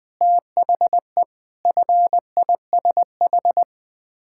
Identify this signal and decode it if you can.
Morse code, heard as a keyed tone, about 20 wpm, 700 Hz. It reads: THE FISH
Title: Sets of 8 Words - 20wpm